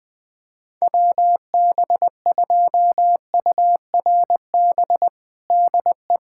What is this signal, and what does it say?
Morse code, heard as a keyed tone, about 20 wpm, 700 Hz. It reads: WB2URB DE
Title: USA Callsign DE Callsign 20wpm